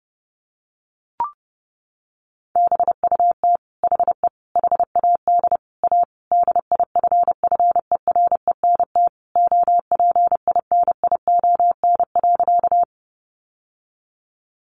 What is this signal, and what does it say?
Morse code, heard as a keyed tone, about 28 wpm, 700 Hz. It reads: BUT HE HAD A DIFFERENT OPINION.
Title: Sentences from Top 1200 Words 30wpm